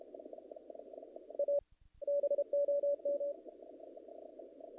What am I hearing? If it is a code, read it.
Morse code, heard as a keyed tone, about 31 wpm, 570 Hz. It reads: A BOM
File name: contests / naqp-cw-k5tr-op-k5pi- -1800